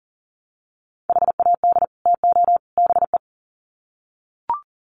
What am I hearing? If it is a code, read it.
Morse code, heard as a keyed tone, about 40 wpm, 700 Hz. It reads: HAD TO BE
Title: Sets of 3 Words 40wpm